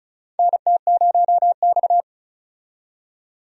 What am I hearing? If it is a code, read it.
Morse code, heard as a keyed tone, about 36 wpm, 700 Hz. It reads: NT0X
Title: USA Call Signs - 4 Characters - 35wpm